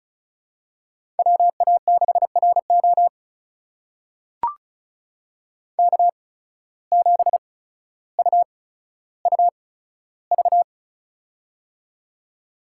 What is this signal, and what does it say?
Morse code, heard as a keyed tone, about 36 wpm, spaced as for 15 wpm, 700 Hz. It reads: WA6RO K7UUV
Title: USA Call Signs - 5 Characters - 8x Character Spacing - 35wpm